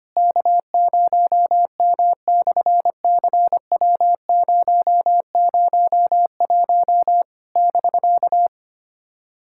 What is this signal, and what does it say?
Morse code, heard as a keyed tone, about 25 wpm, 700 Hz. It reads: K0M/CW001 <BK>
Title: SOTA QSO Practice 25wpm